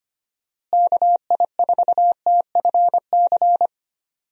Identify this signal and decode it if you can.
Morse code, heard as a keyed tone, about 25 wpm, 700 Hz. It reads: KI4TFC
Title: USA Call Signs - 6 Characters - Repeated 3x - 25wpm